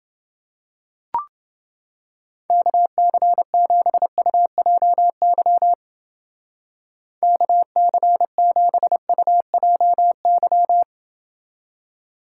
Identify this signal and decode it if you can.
Morse code, heard as a keyed tone, about 29 wpm, 700 Hz. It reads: KC7UJY KC7UJY